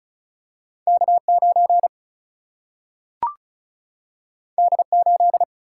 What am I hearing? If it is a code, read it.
Morse code, heard as a keyed tone, about 36 wpm, 700 Hz. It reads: K9 D8
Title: Single-Letter Single-Number 35wpm